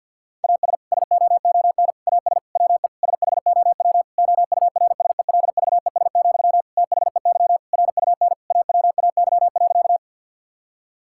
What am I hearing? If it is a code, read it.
Morse code, heard as a keyed tone, about 50 wpm, 700 Hz. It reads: AS SOON AS WE SHOW OURSELVES, THEY RUN AWAY.
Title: Sentences from Top 1600 Words 50wpm